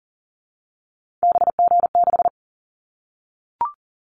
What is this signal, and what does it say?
Morse code, heard as a keyed tone, about 40 wpm, 700 Hz. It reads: BG6
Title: Callsign Prefixes for Top 100 DXCC - 40wpm